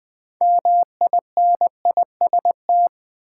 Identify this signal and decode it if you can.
Morse code, heard as a keyed tone, about 20 wpm, 700 Hz. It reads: MINIST
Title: Build A Word - 2 Letters At A Time - 8-Letter Words - 20wpm